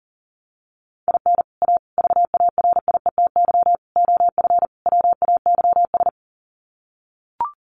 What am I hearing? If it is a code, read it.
Morse code, heard as a keyed tone, about 37 wpm, 700 Hz. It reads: IN A VARIETY OF WAYS
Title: Sets of 5 Words Encore 40wpm